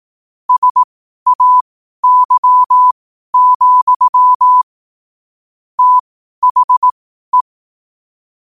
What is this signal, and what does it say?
Morse code, heard as a keyed tone, about 18 wpm, spaced as for 12 wpm, 1000 Hz. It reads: SAY, THE
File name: qotdfiles / 12wpm